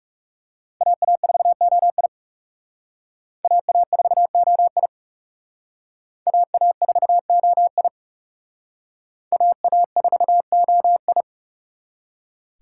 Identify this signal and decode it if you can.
Morse code, heard as a keyed tone, about 37 wpm, 700 Hz. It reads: AA4OI AA4OI AA4OI AA4OI
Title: Speed Racing - USA 2x2 Call Signs 30wpm